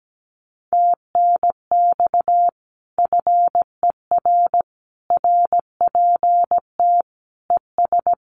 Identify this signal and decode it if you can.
Morse code, heard as a keyed tone, about 17 wpm, 700 Hz. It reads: TNX FER RPT ES
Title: Common QSO Phrases Encore 17wpm